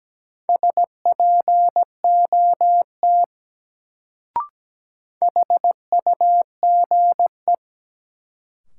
Morse code, heard as a keyed tone, about 17 wpm, 700 Hz. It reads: SPOT HUGE
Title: Sentences from Top 1100 Words - New Word Drill - 17wpm